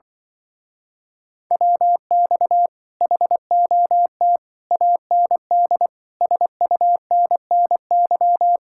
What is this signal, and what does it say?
Morse code, heard as a keyed tone, about 24 wpm, 700 Hz. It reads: WX HOT AND SUNNY
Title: Speed-Racing Common QSO Phrases - 20wpm